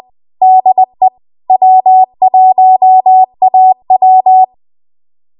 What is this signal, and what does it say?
Morse code, heard as a keyed tone, about 20 wpm, 750 Hz. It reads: DE W1AW
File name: Archive / 20 WPM / 20WPM